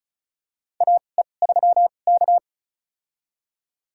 Morse code, heard as a keyed tone, about 34 wpm, spaced as for 25 wpm, 700 Hz. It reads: AE3K
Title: USA Call Signs - 4 Characters - 2x Character Spacing - 35wpm